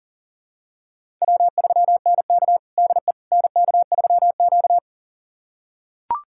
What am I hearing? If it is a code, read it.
Morse code, heard as a keyed tone, about 40 wpm, 700 Hz. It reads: W3NK DE NK3Q
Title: USA Callsign DE Callsign - 4 Characters - 40wpm